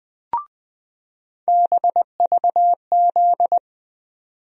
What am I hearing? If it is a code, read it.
Morse code, heard as a keyed tone, about 20 wpm, 700 Hz. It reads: BVZ
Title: USA Call Sign Postfixes - 3 Letters 20wpm